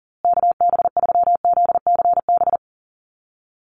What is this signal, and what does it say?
Morse code, heard as a keyed tone, about 40 wpm, 700 Hz. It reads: KB3ZCB
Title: USA Call Signs Repeated 2x 40wpm